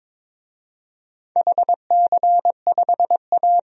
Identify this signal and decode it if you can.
Morse code, heard as a keyed tone, about 22 wpm, 700 Hz. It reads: HC5A
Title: DXCC-100 Call Signs - 4 Characters - 22wpm